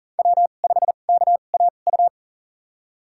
Morse code, heard as a keyed tone, about 40 wpm, spaced as for 29 wpm, 700 Hz. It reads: W5KAU
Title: USA Call Signs - 2x Character Spacing - 40wpm